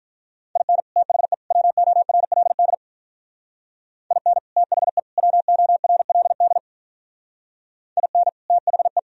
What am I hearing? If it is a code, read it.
Morse code, heard as a keyed tone, about 42 wpm, 700 Hz. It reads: IN THE WORLD IN THE WORLD IN THE
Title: Speed-Racing - Sets of 3 Words 35wpm